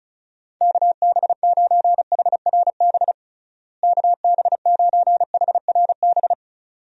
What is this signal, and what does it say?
Morse code, heard as a keyed tone, about 36 wpm, 700 Hz. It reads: KB9HRB KB9HRB
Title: USA Call Signs - 6 Characters - Repeated 2x - 35wpm